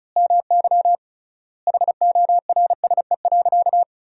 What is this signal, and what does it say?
Morse code, heard as a keyed tone, about 34 wpm, 700 Hz. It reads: MY HORSE.
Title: 3x Word-Spacing - Sentences from Top 500 Words - 35wpm